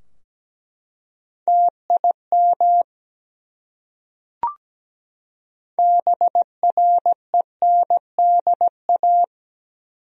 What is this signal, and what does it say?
Morse code, heard as a keyed tone, about 17 wpm, 700 Hz. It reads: TIM BRENDA
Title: USA Common Names - 17wpm